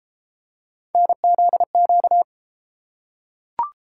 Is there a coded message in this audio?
Yes: Morse code from 0.9 to 2.2 s, heard as a keyed tone, about 30 wpm, 700 Hz. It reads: NZQ